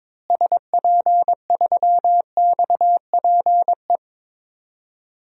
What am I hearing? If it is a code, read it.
Morse code, heard as a keyed tone, about 22 wpm, 700 Hz. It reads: SP3XPE
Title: DXCC-100 Call Signs - 6 Characters - Repeated 2x - 22wpm